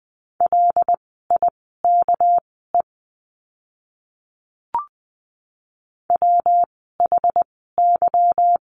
Morse code, heard as a keyed tone, about 20 wpm, spaced as for 15 wpm, 700 Hz. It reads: LIKE WHY
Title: 206 Lesson 33f - Letter j - Words 20wpm